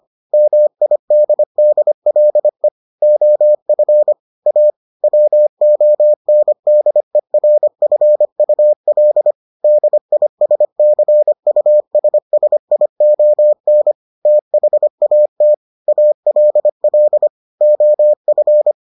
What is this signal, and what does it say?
Morse code, heard as a keyed tone, about 25 wpm, 600 Hz. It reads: MIDDLE OF A WONDERFUL DISCUSSION THAT ALL OF